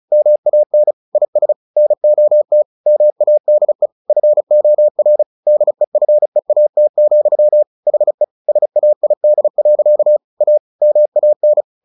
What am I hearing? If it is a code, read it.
Morse code, heard as a keyed tone, about 34 wpm, 600 Hz. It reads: MAN IS NOT MADE FOR DEFEAT, HE SAID. A MAN
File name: the old man and the sea 2 35wpm